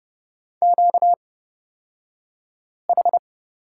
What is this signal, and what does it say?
Morse code, heard as a keyed tone, about 30 wpm, spaced as for 5 wpm, 700 Hz. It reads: QH
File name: Lesson 38d - Letter q - Two Characters 30wpm